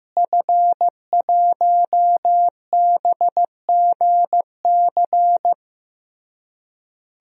Morse code, heard as a keyed tone, about 16 wpm, 700 Hz. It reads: F1BGC
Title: DXCC-100 Call Signs - 5 Characters - Repeated 3x - 15wpm